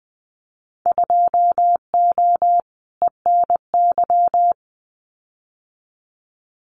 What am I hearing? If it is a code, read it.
Morse code, heard as a keyed tone, about 20 wpm, 700 Hz. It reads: 2O ENY
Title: Winter Field Day Exchanges 20wpm